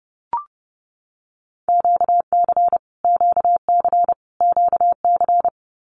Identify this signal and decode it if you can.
Morse code, heard as a keyed tone, about 30 wpm, 700 Hz. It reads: QC QC QC
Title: Canadian Provinces and Territories - Repeated 3x - 30wpm